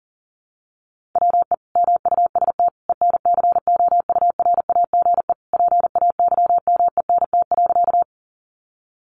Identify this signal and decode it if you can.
Morse code, heard as a keyed tone, about 40 wpm, 700 Hz. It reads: WE MUST ENCOURAGE PAYMENT.
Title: Sentences from Top 1400 Words 40wpm